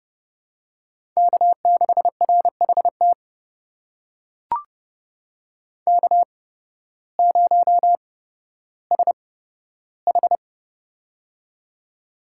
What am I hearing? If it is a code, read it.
Morse code, heard as a keyed tone, about 30 wpm, spaced as for 14 wpm, 700 Hz. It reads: K6RHT K0SH
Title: USA Call Signs - 8x Character Spacing - 30wpm